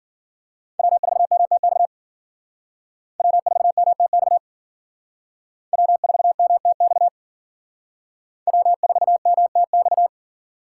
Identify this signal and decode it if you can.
Morse code, heard as a keyed tone, about 53 wpm, 700 Hz. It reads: W4MTX W4MTX W4MTX W4MTX